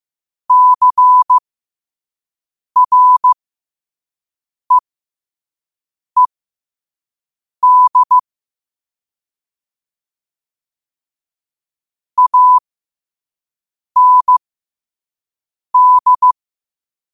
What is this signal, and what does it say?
Morse code, heard as a keyed tone, about 15 wpm, spaced as for 5 wpm, 1000 Hz. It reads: CREED AND
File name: qotdfiles / 5wpm